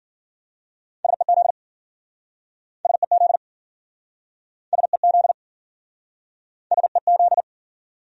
Morse code, heard as a keyed tone, about 52 wpm, 700 Hz. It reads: SEZ SEZ SEZ SEZ